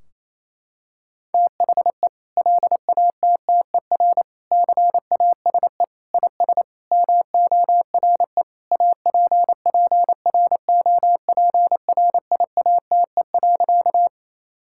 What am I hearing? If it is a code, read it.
Morse code, heard as a keyed tone, about 28 wpm, 700 Hz. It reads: THE LATTER CASE IS MORE APPROPRIATE.
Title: Sentences from Top 1600 Words 28wpm